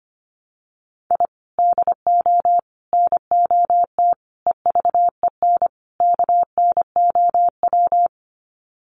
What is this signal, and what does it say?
Morse code, heard as a keyed tone, about 25 wpm, 700 Hz. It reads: I DO NOT EVEN KNOW
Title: Sets of 5 Words 25wpm